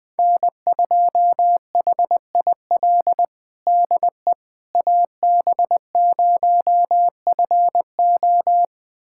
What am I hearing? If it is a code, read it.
Morse code, heard as a keyed tone, about 20 wpm, 700 Hz. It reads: N2HIL DE AB0FO
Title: USA Callsign DE Callsign - 5 Characters - 20wpm